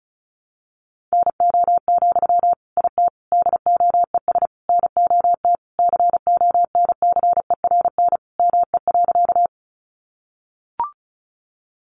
Morse code, heard as a keyed tone, about 34 wpm, 700 Hz. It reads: NO, IT DOES NOT CONCERN ME.
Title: Sentences from Top 1000 Words 35wpm